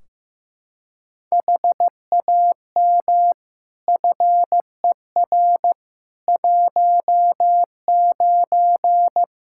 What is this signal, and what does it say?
Morse code, heard as a keyed tone, about 15 wpm, 700 Hz. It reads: HAM FER 19
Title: K1USN SST QSO Practice 15wpm